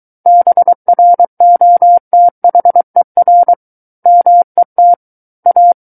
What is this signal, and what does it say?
Morse code, heard as a keyed tone, about 23 wpm, 700 Hz. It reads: BROTHER MET A
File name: Book One / 14-Chapter 14 - 23 WPM